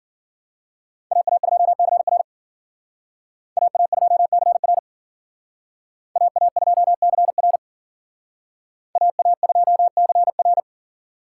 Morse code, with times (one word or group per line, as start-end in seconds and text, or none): 1.1-2.2 AA2CR
3.6-4.8 AA2CR
6.2-7.6 AA2CR
9.0-10.6 AA2CR